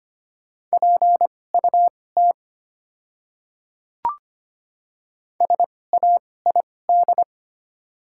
Morse code, heard as a keyed tone, about 25 wpm, spaced as for 18 wpm, 700 Hz. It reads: PUT SAID
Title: 2x Character Spacing - Top 200 Words - 25wpm